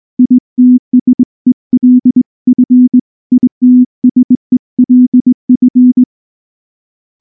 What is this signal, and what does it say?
Morse code, heard as a keyed tone, about 20 wpm, 250 Hz. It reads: ITSELF ITSELF
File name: morse / morse-training-20WPM-4